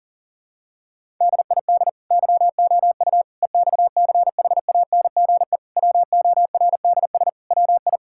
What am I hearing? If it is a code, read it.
Morse code, heard as a keyed tone, about 40 wpm, 700 Hz. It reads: DID YOU EXCHANGE WORDS WI
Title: Sentences from Top 1000 Words - Review Part 6 - 40wpm